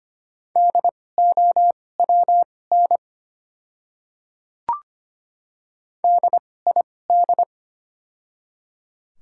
Morse code, heard as a keyed tone, about 25 wpm, spaced as for 18 wpm, 690 Hz. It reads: DOWN DID
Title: 2x Character Spacing - Top 200 Words - 25wpm